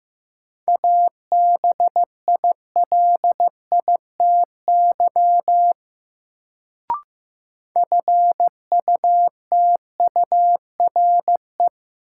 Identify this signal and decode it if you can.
Morse code, heard as a keyed tone, about 15 wpm, 700 Hz. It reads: ABILITY FUTURE
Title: Sets of 3 Words - New Word Drill - 15wpm